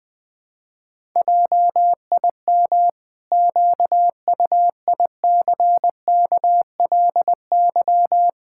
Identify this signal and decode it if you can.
Morse code, heard as a keyed tone, about 20 wpm, 700 Hz. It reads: JIM QUICKLY
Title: Pangram Challenge v2 - 20wpm